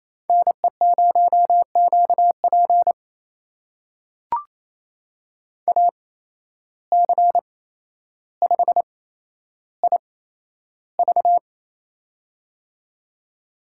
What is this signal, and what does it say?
Morse code, heard as a keyed tone, about 28 wpm, spaced as for 12 wpm, 700 Hz. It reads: NE0QP AC5IV
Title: USA Call Signs - 8x Character Spacing - 28wpm